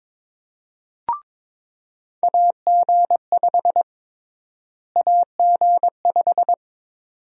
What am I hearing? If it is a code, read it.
Morse code, heard as a keyed tone, about 23 wpm, 700 Hz. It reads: AG5 AG5
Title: USA Call Sign Prefixes - Repeated 2x - 22wpm